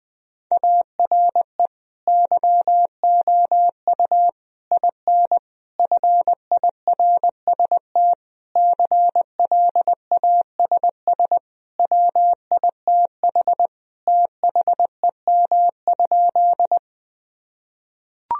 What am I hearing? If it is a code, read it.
Morse code, heard as a keyed tone, about 21 wpm, 700 Hz. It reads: ARE YOU IN FIRST CLASS WITH THEM?
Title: Sentences from Top 1000 Words - Review Part 2 - 20wpm